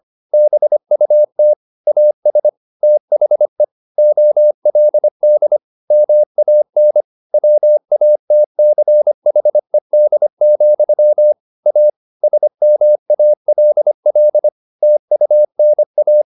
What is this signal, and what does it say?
Morse code, heard as a keyed tone, about 25 wpm, 600 Hz. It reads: BUT AS THE OLD MAN WATCHED, A SMALL TUNA
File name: the old man and the sea part 1 25wpm